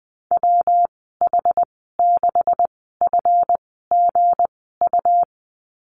Morse code, heard as a keyed tone, about 20 wpm, spaced as for 14 wpm, 700 Hz. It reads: WH6FGU
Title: 193 Lesson 31g - Letter v - Call Signs 20wpm